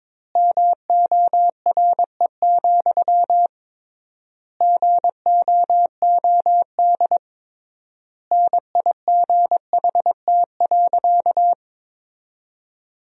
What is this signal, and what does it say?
Morse code, heard as a keyed tone, about 22 wpm, 700 Hz. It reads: MORE, GOOD NIGHT.
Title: 3x Word-Spacing - Sentences from Top 300 Words - 22wpm